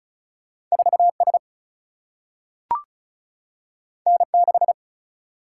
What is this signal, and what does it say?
Morse code, heard as a keyed tone, about 36 wpm, 700 Hz. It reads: 4 S N 6
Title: Single-Letter Single-Number 35wpm